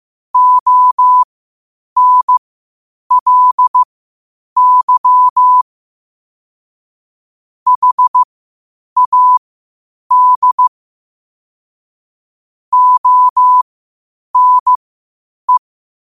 Morse code, heard as a keyed tone, about 15 wpm, spaced as for 8 wpm, 1000 Hz. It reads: ONLY HAD ONE
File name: qotdfiles / 8wpm